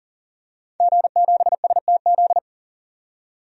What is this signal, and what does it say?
Morse code, heard as a keyed tone, about 39 wpm, 700 Hz. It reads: G7STZ
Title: DXCC-100 Call Signs - 5 Characters - 40wpm